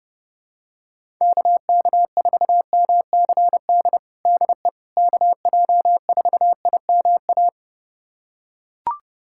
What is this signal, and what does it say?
Morse code, heard as a keyed tone, about 31 wpm, 700 Hz. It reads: KK4MCD DE KJ4IMA
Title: USA Callsign DE Callsign - 6 Characters - 30wpm